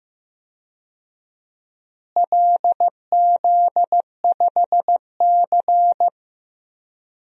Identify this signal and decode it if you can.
Morse code, heard as a keyed tone, about 16 wpm, 700 Hz. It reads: LZ5C
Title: DXCC-100 Call Signs - 4 Characters - Repeated 3x - 15wpm